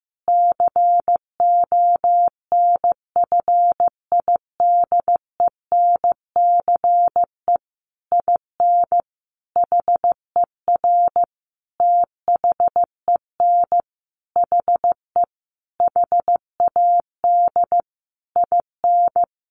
Morse code, heard as a keyed tone, about 15 wpm, 700 Hz. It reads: CONFIDENCE IN HER THEN HE HAD IN
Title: Sentences from Top 1500 Words 15wpm